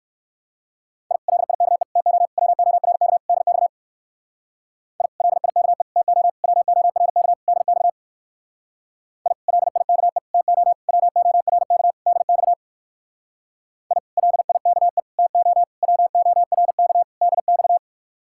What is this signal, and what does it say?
Morse code, heard as a keyed tone, about 59 wpm, 700 Hz. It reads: I LIKE TO WORK DX I LIKE TO WORK DX I LIKE TO WORK DX I LIKE TO WORK DX